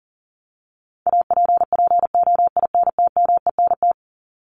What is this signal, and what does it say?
Morse code, heard as a keyed tone, about 40 wpm, 700 Hz. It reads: APPOINTMENT
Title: Top 1751-2000 Words 40wpm